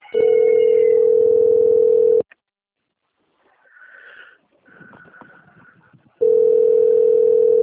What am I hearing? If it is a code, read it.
Ringback tone (North America), 440 + 480 Hz.